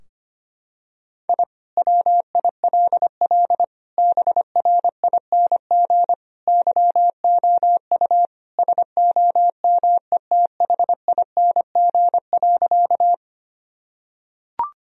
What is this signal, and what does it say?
Morse code, heard as a keyed tone, about 25 wpm, 700 Hz. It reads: I WILL BRING YOU SOMETHING.
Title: Sentences from Top 1000 Words - Review Part 3 - 25wpm